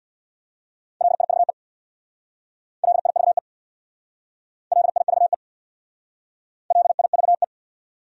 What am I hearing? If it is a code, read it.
Morse code, heard as a keyed tone, about 65 wpm, 700 Hz. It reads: LIVE LIVE LIVE LIVE